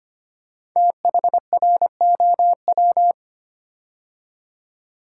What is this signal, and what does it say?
Morse code, heard as a keyed tone, about 25 wpm, 700 Hz. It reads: THROW
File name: 5 Letter Words - First 3 Letters Given - 25wpm